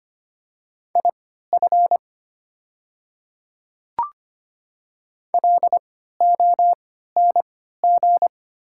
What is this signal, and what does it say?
Morse code, heard as a keyed tone, about 25 wpm, spaced as for 14 wpm, 700 Hz. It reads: IF LONG
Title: 3x Character Spacing - Top 100 Words - 25wpm